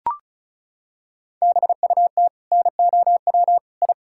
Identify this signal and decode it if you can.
Morse code, heard as a keyed tone, about 36 wpm, 700 Hz. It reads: BUT NOW I
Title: Sentences from Top 1000 Words - Review Part 6 - 35wpm